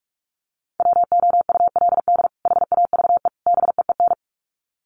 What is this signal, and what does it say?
Morse code, heard as a keyed tone, about 44 wpm, 700 Hz. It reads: WOULD HAVE BEEN
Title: Sets of 3 Words 45wpm